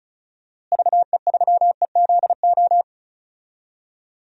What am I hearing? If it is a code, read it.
Morse code, heard as a keyed tone, about 36 wpm, 700 Hz. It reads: VE3EZO
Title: DXCC-100 Call Signs - 6 Characters - Repeated 3x - 35wpm